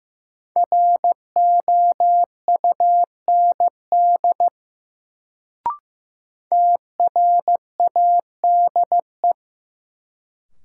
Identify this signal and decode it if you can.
Morse code, heard as a keyed tone, about 15 wpm, 700 Hz. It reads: ROUND TRADE
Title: Sets of 5 Words Encore - New Word Drill - 15wpm